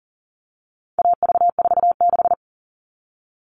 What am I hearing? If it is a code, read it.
Morse code, heard as a keyed tone, about 40 wpm, 700 Hz. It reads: AV46
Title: All Cut Numbers - 4-Digit 40wpm